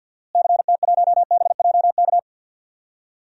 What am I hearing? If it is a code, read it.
Morse code, heard as a keyed tone, about 49 wpm, 700 Hz. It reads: CT1BJK
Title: DXCC-100 Call Signs - 6 Characters - Repeated 2x - 50wpm